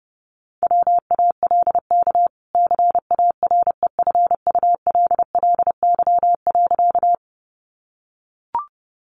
Morse code, heard as a keyed tone, about 30 wpm, 700 Hz. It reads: WALK CAREFULLY.
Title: Sentences from Top 1300 Words 30wpm